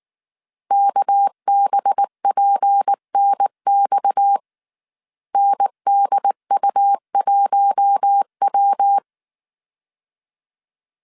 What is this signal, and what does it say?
Morse code, heard as a keyed tone, about 19 wpm, 800 Hz. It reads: KBPNX NDU1W
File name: CW / DET / 1 6 1500C S A 18wpm 2255